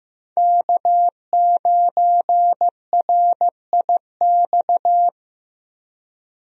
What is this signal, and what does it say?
Morse code, heard as a keyed tone, about 15 wpm, 700 Hz. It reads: K9RIX
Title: USA Call Signs - 5 Characters - Repeated 2x - 15wpm